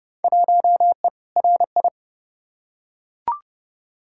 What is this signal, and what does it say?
Morse code, heard as a keyed tone, about 30 wpm, 700 Hz. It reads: 1E RI
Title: Field Day Exchanges Encore - 30wpm